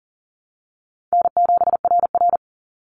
Morse code, heard as a keyed tone, about 40 wpm, 700 Hz. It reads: N7RR
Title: NA SOTA Activator Callsigns 40wpm